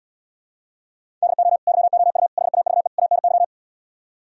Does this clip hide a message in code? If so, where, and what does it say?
Yes: Morse code from 1.2 to 3.4 s, heard as a keyed tone, about 77 wpm, 700 Hz. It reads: DO YOU HAVE ANY